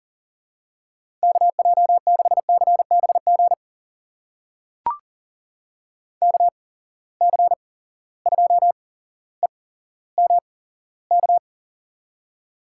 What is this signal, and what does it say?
Morse code, heard as a keyed tone, about 40 wpm, spaced as for 17 wpm, 700 Hz. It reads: KJ6CBG KC2EMK
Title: USA Call Signs - 6 Characters - 8x Character Spacing - 40wpm